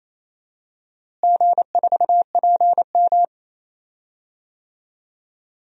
Morse code, heard as a keyed tone, about 28 wpm, 700 Hz. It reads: G4PM
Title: DXCC-100 Call Signs - 4 Characters - Repeated 3x - 28wpm